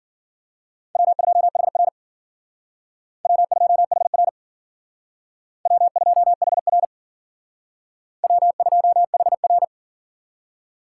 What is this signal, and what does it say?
Morse code, heard as a keyed tone, about 53 wpm, 700 Hz. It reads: W2HR W2HR W2HR W2HR